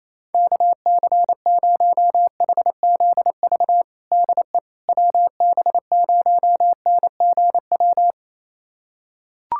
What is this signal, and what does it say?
Morse code, heard as a keyed tone, about 28 wpm, 700 Hz. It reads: KC0HZV DE WB0NGW
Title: USA Callsign DE Callsign - 6 Characters - 28wpm